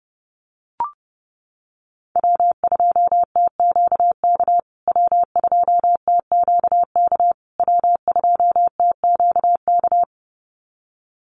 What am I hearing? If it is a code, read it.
Morse code, heard as a keyed tone, about 30 wpm, 700 Hz. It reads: W2TQK W2TQK W2TQK
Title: USA Call Signs Repeated 3x 30wpm